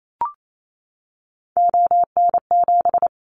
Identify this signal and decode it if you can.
Morse code, heard as a keyed tone, about 28 wpm, 700 Hz. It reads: ON7
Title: Callsign Prefixes for Top 50 DXCC - 28wpm